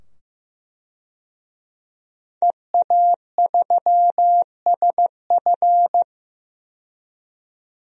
Morse code, heard as a keyed tone, about 15 wpm, 700 Hz. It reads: EA3SF
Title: DXCC-100 Call Signs - 5 Characters - 15wpm